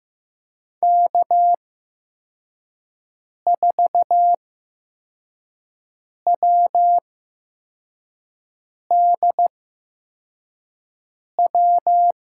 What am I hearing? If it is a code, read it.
Morse code, heard as a keyed tone, about 15 wpm, spaced as for 4 wpm, 700 Hz. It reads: K4WDW
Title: USA Call Signs - 8x Character Spacing - 15wpm